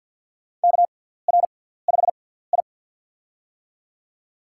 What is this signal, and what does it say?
Morse code, heard as a keyed tone, about 46 wpm, spaced as for 17 wpm, 700 Hz. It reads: KR5I
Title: USA Call Signs - 4 Characters - 6x Character Spacing - 50wpm